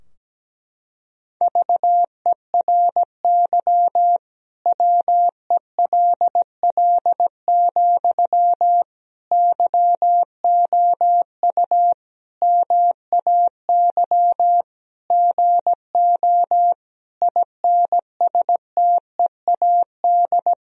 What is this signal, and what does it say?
Morse code, heard as a keyed tone, about 17 wpm, 700 Hz. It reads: VERY WELL, YOU MAY GO INSTEAD
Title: Sentences from Top 1000 Words - Review Part 4 - 17wpm